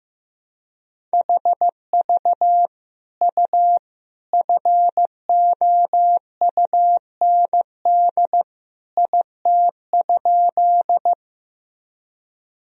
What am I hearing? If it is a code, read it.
Morse code, heard as a keyed tone, about 15 wpm, 700 Hz. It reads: HV U FOUND IT?
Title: Sentences from Top 300 Words and Abbreviations 15wpm